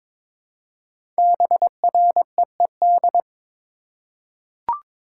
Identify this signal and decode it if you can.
Morse code, heard as a keyed tone, about 23 wpm, 700 Hz. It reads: BREED
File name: 5 Letter Words - First 3 Letters Given - 22wpm